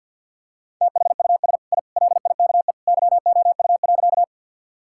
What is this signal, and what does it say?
Morse code, heard as a keyed tone, about 50 wpm, 700 Hz. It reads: THUS I LIKE YOU.
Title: Sentences from Top 1000 Words - Review Part 5 - 50wpm